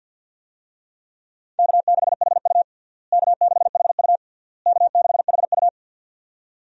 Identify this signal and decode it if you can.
Morse code, heard as a keyed tone, about 52 wpm, 700 Hz. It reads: K6HU K6HU K6HU
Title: USA Call Signs - 4 Characters - Repeated 3x - 50wpm